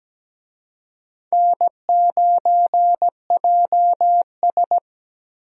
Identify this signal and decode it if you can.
Morse code, heard as a keyed tone, about 17 wpm, 700 Hz. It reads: N9JS
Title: USA Call Signs - 4 Characters - 17wpm